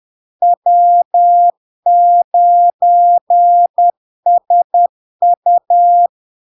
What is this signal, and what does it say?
Morse code, heard as a keyed tone, about 10 wpm, 700 Hz. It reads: W9SU
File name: LARK-FD-2022-Log-in-CW-10wpm-700-Hz